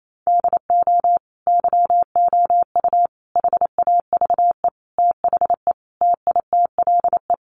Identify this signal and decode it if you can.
Morse code, heard as a keyed tone, about 28 wpm, 700 Hz. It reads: DO YOU HAVE THE TITLE
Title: Sentences from Top 1200 Words 28wpm